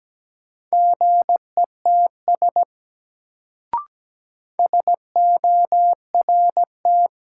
Morse code, heard as a keyed tone, about 17 wpm, 700 Hz. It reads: GETS SORT
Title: Sentences from Top 1300 Words - New Word Drill Alt - 17wpm